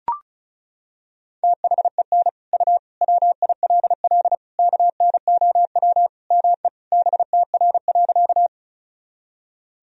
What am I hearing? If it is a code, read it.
Morse code, heard as a keyed tone, about 36 wpm, 700 Hz. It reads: THEN U WILL KNOW ME BTR.
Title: Sentences from Top 400 Words and Abbreviations 35wpm